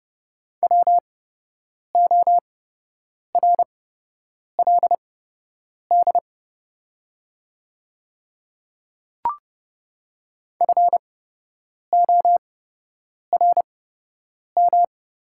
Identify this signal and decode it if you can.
Morse code, heard as a keyed tone, about 30 wpm, spaced as for 8 wpm, 700 Hz. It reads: WORLD FORM
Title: 8x Character Spacing - Top 200 Words - 30wpm